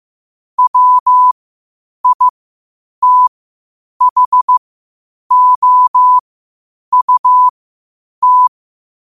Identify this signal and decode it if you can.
Morse code, heard as a keyed tone, about 15 wpm, spaced as for 8 wpm, 1000 Hz. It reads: WITHOUT
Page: New quotes every day in morse code at 8 Words per minute.